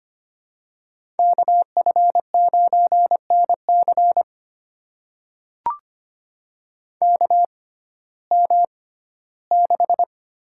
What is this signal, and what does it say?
Morse code, heard as a keyed tone, about 25 wpm, spaced as for 15 wpm, 700 Hz. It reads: KF9NC KM6
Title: USA Call Signs - 5 Characters - 6x Character Spacing - 25wpm